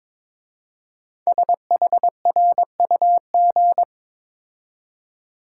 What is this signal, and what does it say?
Morse code, heard as a keyed tone, about 23 wpm, 700 Hz. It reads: SHRUG
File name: 5 Letter Words - First 3 Letters Given - 22wpm